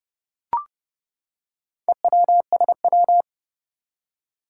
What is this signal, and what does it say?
Morse code, heard as a keyed tone, about 30 wpm, 700 Hz. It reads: EWSW
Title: Four Challenging Letter-Number Combinations 30wpm